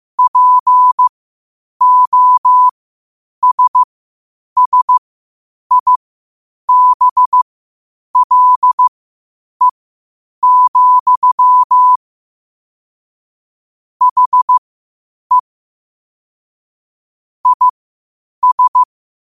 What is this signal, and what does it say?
Morse code, heard as a keyed tone, about 15 wpm, spaced as for 8 wpm, 1000 Hz. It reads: POSSIBLE, HE IS
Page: New quotes every day in morse code at 8 Words per minute.